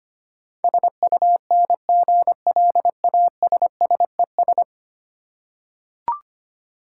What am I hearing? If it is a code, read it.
Morse code, heard as a keyed tone, about 25 wpm, 690 Hz. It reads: SUNGLASSES
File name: practice / 25wpm / Easy Words with Hints - 25wpm